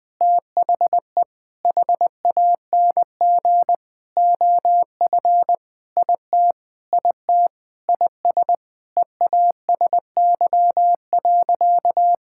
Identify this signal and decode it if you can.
Morse code, heard as a keyed tone, about 21 wpm, 700 Hz. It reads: THE HANG OF IT IT IS EASY.
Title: Sentences with 7-Word Phrases - 20wpm